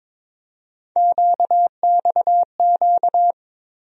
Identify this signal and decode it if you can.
Morse code, heard as a keyed tone, about 22 wpm, 700 Hz. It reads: QXQ
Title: USA Call Sign Postfixes - 3 Letters 22wpm